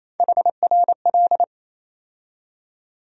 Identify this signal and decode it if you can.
Morse code, heard as a keyed tone, about 28 wpm, 700 Hz. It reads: HRL
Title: USA Call Sign Postfixes - 3 Letters 28wpm